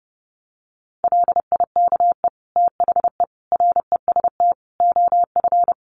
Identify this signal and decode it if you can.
Morse code, heard as a keyed tone, about 30 wpm, 700 Hz. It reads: LIKE THE REST OF
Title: Sentences from Top 1400 Words 30wpm